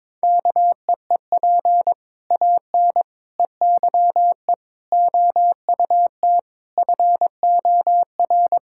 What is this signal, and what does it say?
Morse code, heard as a keyed tone, about 22 wpm, 700 Hz. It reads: KEEP AN EYE OUT FOR
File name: practice / 22wpm / Sets of 7 Words - 22wpm